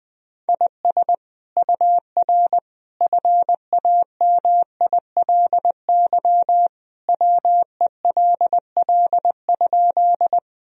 Morse code, heard as a keyed tone, about 20 wpm, 700 Hz. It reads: IS UR FAMILY WELL?
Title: Sentences from Top 400 Words and Abbreviations 20wpm